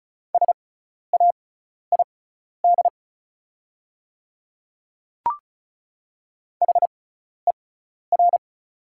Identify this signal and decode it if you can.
Morse code, heard as a keyed tone, about 36 wpm, spaced as for 12 wpm, 690 Hz. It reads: SAID HER
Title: 6x Character Spacing - Top 300 Words - 35wpm